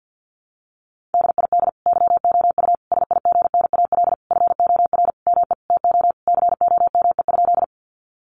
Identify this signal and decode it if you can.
Morse code, heard as a keyed tone, about 50 wpm, 700 Hz. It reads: DID YOU SIGNAL FOR ME TO COME?
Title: Sentences from Top 1400 Words 50wpm